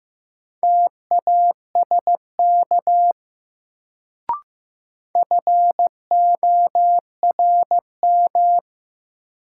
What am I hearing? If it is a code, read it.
Morse code, heard as a keyed tone, about 15 wpm, 700 Hz. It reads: TASK FORM
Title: Sentences from Top 1100 Words - New Word Drill Alt - 15wpm